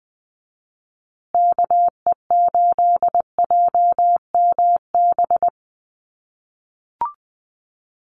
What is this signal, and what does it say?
Morse code, heard as a keyed tone, about 20 wpm, 700 Hz. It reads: KE8JMB
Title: USA Call Signs Repeated 3x 20wpm